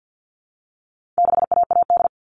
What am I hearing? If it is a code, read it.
Morse code, heard as a keyed tone, about 50 wpm, 700 Hz. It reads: 6AAD
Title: All Cut Numbers - 4-Digit 50wpm